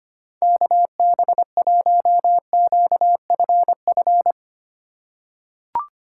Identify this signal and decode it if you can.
Morse code, heard as a keyed tone, about 25 wpm, 700 Hz. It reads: KB1QFF
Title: USA Call Signs - 6 Characters - 3x Character Spacing - 25wpm